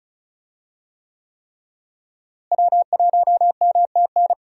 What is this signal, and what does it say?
Morse code, heard as a keyed tone, about 36 wpm, 700 Hz. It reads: W1MTN
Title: USA Call Signs - 5 Characters - Repeated 3x - 35wpm